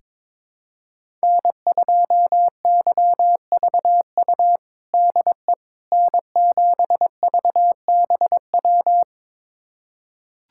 Morse code, heard as a keyed tone, about 22 wpm, 700 Hz. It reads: N2YVU DE N7VBW
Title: USA Callsign DE Callsign 22wpm